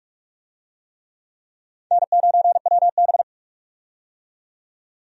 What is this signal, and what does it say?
Morse code, heard as a keyed tone, about 46 wpm, 700 Hz. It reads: N9WB
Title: USA Call Signs - 4 Characters - 45wpm